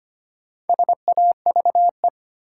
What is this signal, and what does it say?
Morse code, heard as a keyed tone, about 25 wpm, 700 Hz. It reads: SAVE
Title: Sentences from Top 1100 Words - New Word Drill Alt - 25wpm